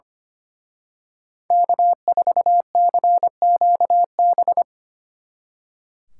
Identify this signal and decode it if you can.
Morse code, heard as a keyed tone, about 29 wpm, 700 Hz. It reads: K4CQB
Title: Speed Racing - USA 1x3 Call Signs 25wpm